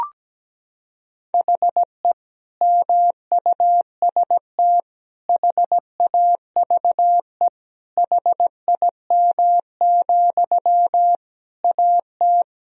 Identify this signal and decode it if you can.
Morse code, heard as a keyed tone, about 17 wpm, 700 Hz. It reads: HE MUST HAVE HIM, AT
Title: Sentences from Top 1000 Words - Review Part 4 - 17wpm